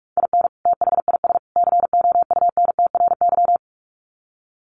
Morse code, heard as a keyed tone, about 44 wpm, 700 Hz. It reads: IN THIS COUNTRY
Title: Sets of 3 Words 45wpm